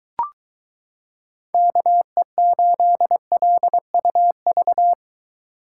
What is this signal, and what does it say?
Morse code, heard as a keyed tone, about 23 wpm, 700 Hz. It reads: KE8LUV